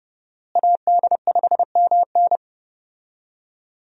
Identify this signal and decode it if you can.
Morse code, heard as a keyed tone, about 29 wpm, 700 Hz. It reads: AD5MN
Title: USA Call Signs - 5 Characters - Repeated 3x - 30wpm